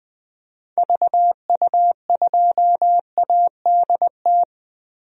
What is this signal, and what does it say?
Morse code, heard as a keyed tone, about 20 wpm, 700 Hz. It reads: VU2ADT
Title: DXCC-100 Call Signs - 6 Characters - Repeated 2x - 20wpm